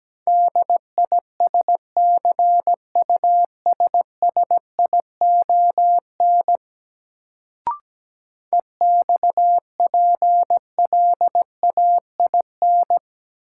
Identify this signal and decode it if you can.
Morse code, heard as a keyed tone, about 17 wpm, 700 Hz. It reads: DISCUSSION EXPLAIN
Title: Sentences from Top 1100 Words - New Word Drill - 17wpm